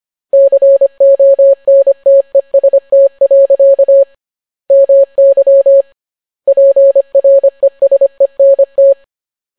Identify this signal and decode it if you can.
Morse code, heard as a keyed tone, about 25 wpm, 550 Hz. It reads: CONTEST. MY PRESENT